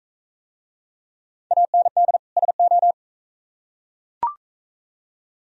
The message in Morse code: AND SO

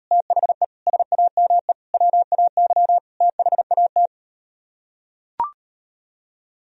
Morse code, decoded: THE SAME WAY THAT

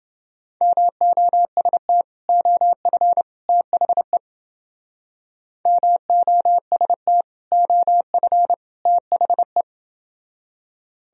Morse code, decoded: MOST OF THE MOST OF THE

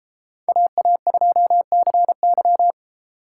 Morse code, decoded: AA2CY

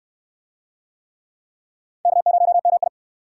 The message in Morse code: K0MI

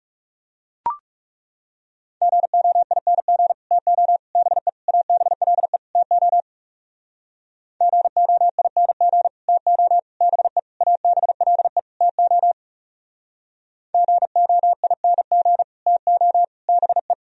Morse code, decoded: GOING TO BE ABLE TO GOING TO BE ABLE TO GOING TO BE

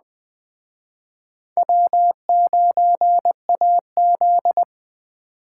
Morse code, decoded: W9AZ

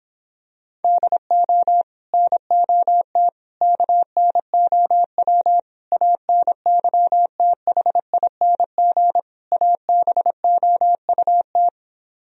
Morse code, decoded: DO NOT KNOW ANYTHING ABOUT

